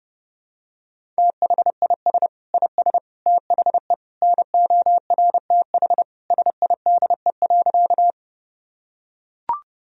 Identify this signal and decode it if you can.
Morse code, heard as a keyed tone, about 30 wpm, 700 Hz. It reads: THIS IS THE NORTH SIDE.
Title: Sentences from Top 1000 Words - Review Part 1 - 30wpm